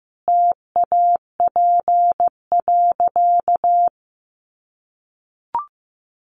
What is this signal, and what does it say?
Morse code, heard as a keyed tone, about 15 wpm, 700 Hz. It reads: TAP.
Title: Sentences from Top 1400 Words 15wpm